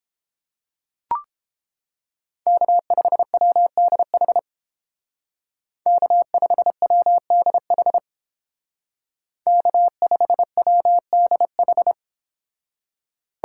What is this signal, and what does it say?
Morse code, heard as a keyed tone, about 31 wpm, 700 Hz. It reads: K5WDH K5WDH K5WDH